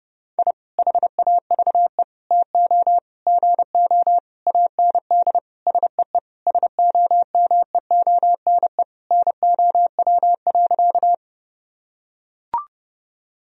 Morse code, heard as a keyed tone, about 30 wpm, 700 Hz. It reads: I HAVE TO GO AND SEE SOMEONE NOW.
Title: Sentences from Top 1000 Words - Review Part 4 - 30wpm